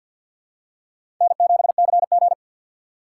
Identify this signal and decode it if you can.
Morse code, heard as a keyed tone, about 53 wpm, 700 Hz. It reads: N7CG
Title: USA Call Signs - 4 Characters - 2x Character Spacing - 50wpm